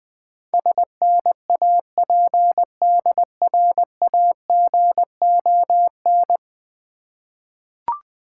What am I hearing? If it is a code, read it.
Morse code, heard as a keyed tone, about 20 wpm, 690 Hz. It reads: SNAPDRAGON
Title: Difficult Words with Hints - 20wpm